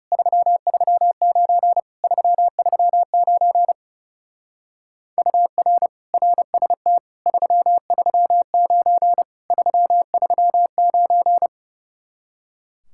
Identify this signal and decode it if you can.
Morse code, heard as a keyed tone, about 35 wpm, 700 Hz. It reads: 339 339 UR RST 339 339
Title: Speed-Racing Common QSO Phrases - 30wpm